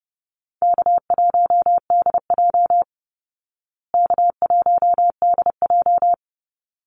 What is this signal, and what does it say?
Morse code, heard as a keyed tone, about 29 wpm, 700 Hz. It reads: K1DJ K1DJ
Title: Call Signs Repeated 3x - 30wpm